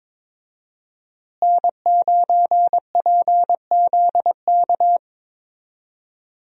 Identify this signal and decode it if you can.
Morse code, heard as a keyed tone, about 22 wpm, 690 Hz. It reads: N9PZK
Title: USA Call Signs - 5 Characters - Repeated 3x - 22wpm